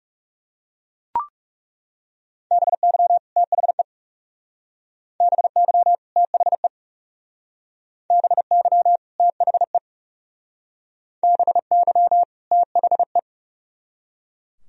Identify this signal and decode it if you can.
Morse code, heard as a keyed tone, about 37 wpm, 700 Hz. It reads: BY THE BY THE BY THE BY THE
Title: Speed-Racing - Sets of 2 Words 30wpm